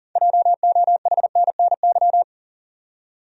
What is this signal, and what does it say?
Morse code, heard as a keyed tone, about 40 wpm, 700 Hz. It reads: JOHNNY
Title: USA Common Names - 40wpm